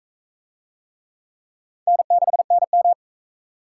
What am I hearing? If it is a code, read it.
Morse code, heard as a keyed tone, about 40 wpm, 700 Hz. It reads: N6NM